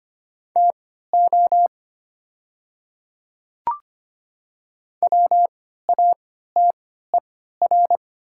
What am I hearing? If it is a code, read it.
Morse code, heard as a keyed tone, about 25 wpm, spaced as for 14 wpm, 700 Hz. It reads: TO WATER
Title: 3x Character Spacing - Top 100 Words - 25wpm